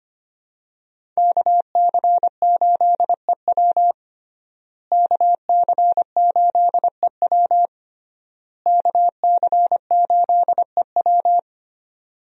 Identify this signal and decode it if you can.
Morse code, heard as a keyed tone, about 25 wpm, 700 Hz. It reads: KC8EW KC8EW KC8EW
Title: USA Call Signs - 5 Characters - Repeated 3x - 25wpm